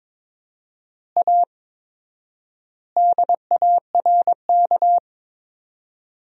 Morse code, heard as a keyed tone, about 21 wpm, 700 Hz. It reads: A DARK
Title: 4x Word-Spacing - Sentences from Top 500 Words - 22wpm